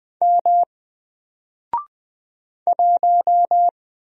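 Morse code, heard as a keyed tone, about 20 wpm, 700 Hz. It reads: M 1
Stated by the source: Single letter and number Morse drill at 20 wpm, from N4LCD.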